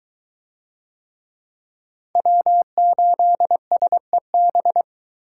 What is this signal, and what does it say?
Morse code, heard as a keyed tone, about 20 wpm, 700 Hz. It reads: W8SEB